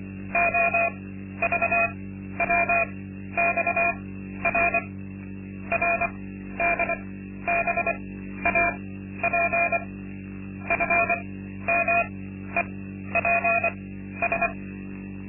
Morse code, heard as a keyed tone, about 24 wpm, spaced as for 12 wpm, 680 Hz. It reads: OVWXR RDBAP FMEPS
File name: Linksmakalnis-2